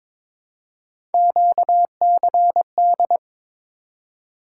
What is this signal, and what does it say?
Morse code, heard as a keyed tone, about 22 wpm, 700 Hz. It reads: QCD
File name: practice / 22wpm / Three Letter-Number Combinations - 22wpm